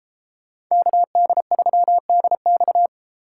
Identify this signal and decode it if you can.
Morse code, heard as a keyed tone, about 36 wpm, 700 Hz. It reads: KD3DX